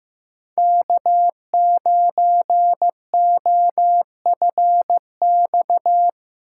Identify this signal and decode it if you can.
Morse code, heard as a keyed tone, about 18 wpm, 700 Hz. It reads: K9OFX
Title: Speed Racing - USA 1x3 Call Signs 15wpm